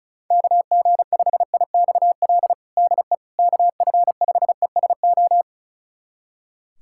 Morse code, heard as a keyed tone, about 36 wpm, 700 Hz. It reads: KG5IXL DE KF5ESO
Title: USA Callsign DE Callsign - 6 Characters - 35wpm